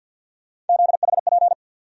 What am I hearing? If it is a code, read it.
Morse code, heard as a keyed tone, about 47 wpm, 700 Hz. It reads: ZHP
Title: USA Call Sign Postfixes - 3 Letters 50wpm